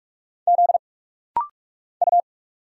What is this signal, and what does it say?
Morse code, heard as a keyed tone, about 46 wpm, 700 Hz. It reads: Z U
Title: Single Letters - Rapid-Fire 45wpm